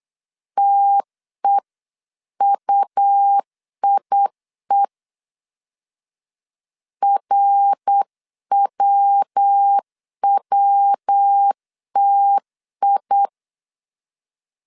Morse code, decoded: TEUIE RWWTI

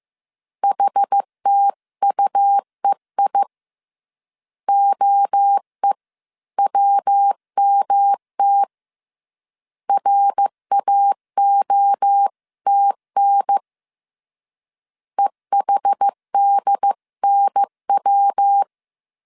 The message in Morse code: HTUEI OEWMT RAOTN EHDNW

